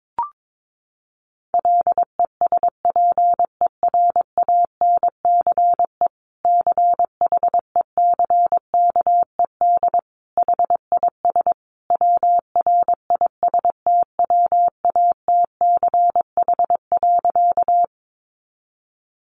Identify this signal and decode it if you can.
Morse code, heard as a keyed tone, about 22 wpm, 700 Hz. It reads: LESPERANCE CHECKED HIS WRISTWATCH.
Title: A Sound Of Thunder 22wpm